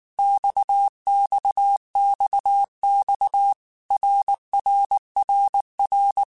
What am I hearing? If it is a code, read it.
Morse code, heard as a keyed tone, about 19 wpm, 800 Hz. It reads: XXXX RRRR
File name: Basic Letter Groups By7Groups / RKPX / RKPX@18